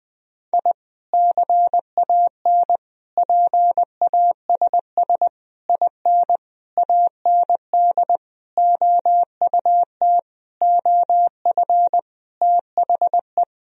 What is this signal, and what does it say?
Morse code, heard as a keyed tone, about 20 wpm, 700 Hz. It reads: I CAN PASS IN AND OUT OF THE
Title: Sentences from Top 1000 Words - Review Part 4 - 20wpm